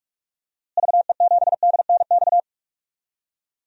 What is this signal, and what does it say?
Morse code, heard as a keyed tone, about 42 wpm, 700 Hz. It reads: VE7DNX
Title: DXCC-100 Call Signs - 6 Characters - Repeated 2x - 45wpm